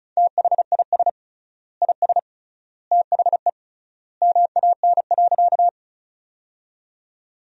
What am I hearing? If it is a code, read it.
Morse code, heard as a keyed tone, about 34 wpm, 700 Hz. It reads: THIS IS THE MAN.
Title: 3x Word-Spacing - Sentences from Top 200 Words - 35wpm